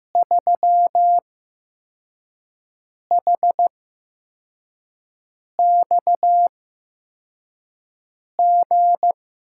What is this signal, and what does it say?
Morse code, heard as a keyed tone, about 15 wpm, spaced as for 4 wpm, 700 Hz. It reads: 3HXG
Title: USA Call Signs - 5 Characters - 8x Character Spacing - 15wpm